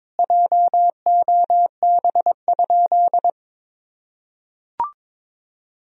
Morse code, heard as a keyed tone, about 22 wpm, 700 Hz. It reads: JOB?
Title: Sentences from Top 1000 Words - Review Part 3 - 22wpm